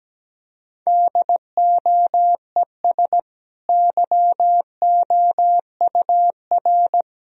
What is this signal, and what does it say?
Morse code, heard as a keyed tone, about 17 wpm, 700 Hz. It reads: DOES YOUR
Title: Sentences from Top 1000 Words - Review Part 3 - 17wpm